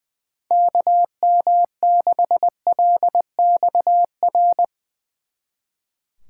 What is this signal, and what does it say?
Morse code, heard as a keyed tone, about 22 wpm, 700 Hz. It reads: KM6LXR
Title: Speed Racing - USA 2x3 Call Signs 20wpm